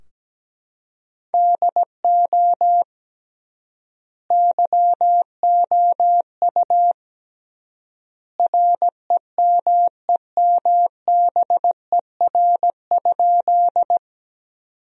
Morse code, heard as a keyed tone, about 17 wpm, 700 Hz. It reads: DO YOU REMEMBER?
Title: 3x Word-Spacing - Sentences from Top 400 Words - 17wpm